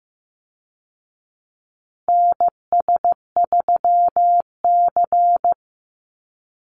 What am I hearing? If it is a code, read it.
Morse code, heard as a keyed tone, about 15 wpm, 700 Hz. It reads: NS3C
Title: Call Signs Repeated 2x - 15wpm